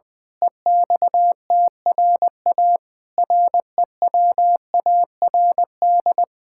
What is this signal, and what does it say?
Morse code, heard as a keyed tone, about 21 wpm, 700 Hz. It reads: EXTRA REWARD
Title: Pangram Challenge v2 - 20wpm